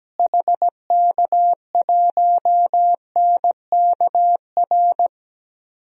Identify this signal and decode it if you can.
Morse code, heard as a keyed tone, about 17 wpm, 700 Hz. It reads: HK1NKR
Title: DXCC-100 Call Signs - 6 Characters - Repeated 2x - 17wpm